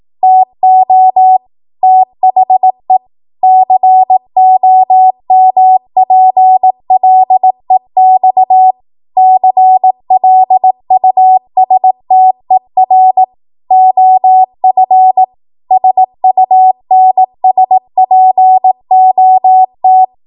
Morse code, heard as a keyed tone, about 18 wpm, 750 Hz. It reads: TO THE COMPLEX CLUSTER OF SUNSPOT
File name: Morse / 18WPM